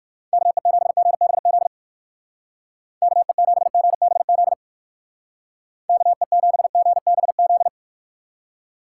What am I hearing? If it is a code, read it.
Morse code, heard as a keyed tone, about 54 wpm, 700 Hz. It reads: KE7GBZ KE7GBZ KE7GBZ
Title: Speed Racing - USA 2x3 Call Signs 45wpm